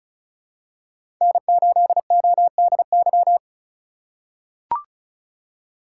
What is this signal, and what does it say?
Morse code, heard as a keyed tone, about 34 wpm, 700 Hz. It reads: N8ODY
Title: USA Call Signs - 5 Characters - 3x Character Spacing - 35wpm